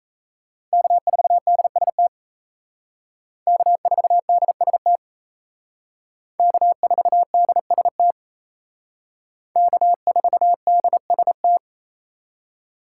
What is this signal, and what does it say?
Morse code, heard as a keyed tone, about 34 wpm, 700 Hz. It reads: K4DST K4DST K4DST K4DST
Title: Speed Racing - USA Call Signs 28wpm